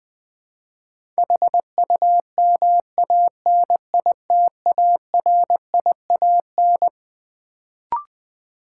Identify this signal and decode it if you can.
Morse code, heard as a keyed tone, about 20 wpm, 700 Hz. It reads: HUMANITARIAN
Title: Build A Word - 6 Letters At A Time - 12-Letter Words - 20wpm